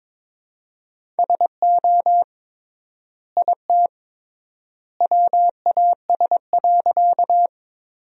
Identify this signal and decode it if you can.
Morse code, heard as a keyed tone, about 22 wpm, 700 Hz. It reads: SO IT WAS.
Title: 3x Word-Spacing - Sentences from Top 100 Words - 22wpm